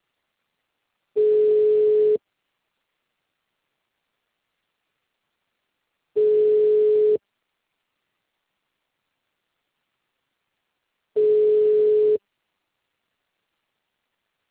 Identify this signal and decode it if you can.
Ringback tone (Europe, 425 Hz).